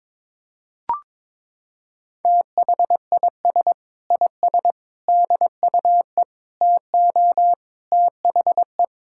Morse code, heard as a keyed tone, about 22 wpm, 700 Hz. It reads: THIS IS DUE TO THE
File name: Sentences with 7-Word Phrases - 22wpm